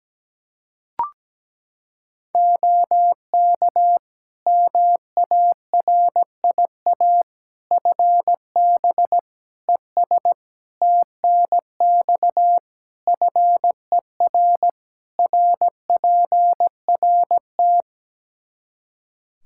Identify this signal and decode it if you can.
Morse code, heard as a keyed tone, about 17 wpm, 700 Hz. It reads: OK MARIA FB ES TNX FER RPRT
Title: Practice QSO - Exchange 2 - Info - 17wpm